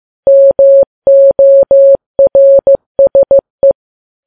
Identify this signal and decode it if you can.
Morse code, heard as a keyed tone, about 15 wpm, 550 Hz. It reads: MORSE
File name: Files Web tls / Morse on9fv / morse15kps